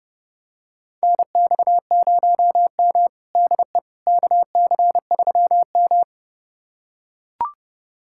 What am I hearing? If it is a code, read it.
Morse code, heard as a keyed tone, about 30 wpm, 700 Hz. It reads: NX0M DE KC3M
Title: USA Callsign DE Callsign - 4 Characters - 30wpm